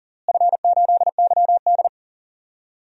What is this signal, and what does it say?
Morse code, heard as a keyed tone, about 40 wpm, 700 Hz. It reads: F8YD
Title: DXCC-100 Call Signs - 4 Characters - Repeated 2x - 40wpm